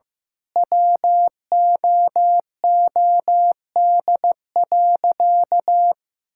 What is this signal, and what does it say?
Morse code, heard as a keyed tone, about 15 wpm, 700 Hz. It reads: WOOD.
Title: Sentences from Top 1600 Words 15wpm